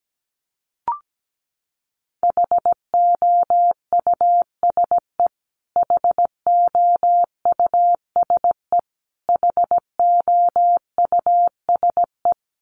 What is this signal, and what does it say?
Morse code, heard as a keyed tone, about 17 wpm, 700 Hz. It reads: HOUSE HOUSE HOUSE
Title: Top 300 Words Repeated 17wpm